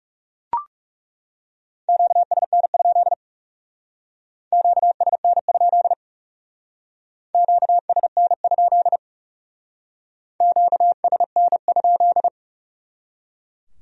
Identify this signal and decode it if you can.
Morse code, heard as a keyed tone, about 38 wpm, 700 Hz. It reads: QSN? QSN? QSN? QSN?
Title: Speed-Racing Q-Codes - 30wpm